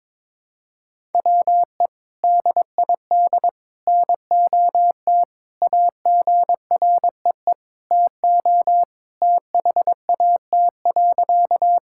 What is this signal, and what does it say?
Morse code, heard as a keyed tone, about 22 wpm, 700 Hz. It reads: WE DID NOT AGREE TO THAT.
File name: practice / 22wpm / Sentences with 3-Word Phrases - 22wpm